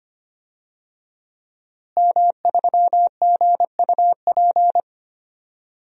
Morse code, heard as a keyed tone, about 25 wpm, 700 Hz. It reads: M3GUP
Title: DXCC-100 Call Signs - 5 Characters - Repeated 2x - 25wpm